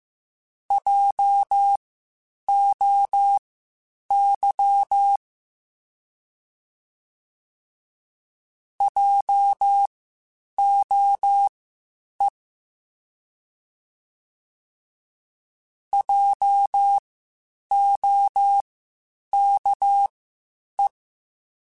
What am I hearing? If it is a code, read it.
Morse code, heard as a keyed tone, about 15 wpm, spaced as for 8 wpm, 800 Hz. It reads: JOY JOE JOKE
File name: Spaced Alphas / Spaced Alphas mp3s / Js sp @13